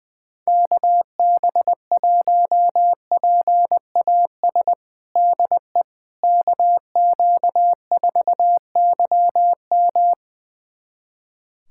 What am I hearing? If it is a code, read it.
Morse code, heard as a keyed tone, about 20 wpm, 700 Hz. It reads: KB1PAS DE KQ4YM
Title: USA Callsign DE Callsign 20wpm